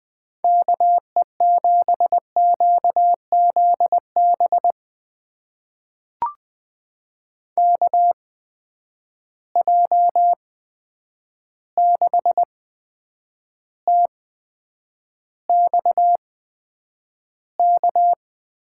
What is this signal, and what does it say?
Morse code, heard as a keyed tone, about 20 wpm, spaced as for 9 wpm, 700 Hz. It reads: KE7QZB KJ6TXK
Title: USA Call Signs - 6 Characters - 8x Character Spacing - 20wpm